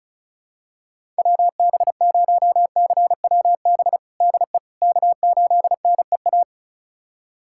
Morse code, heard as a keyed tone, about 36 wpm, 700 Hz. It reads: WB0CWB DE K8NEA
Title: USA Callsign DE Callsign 35wpm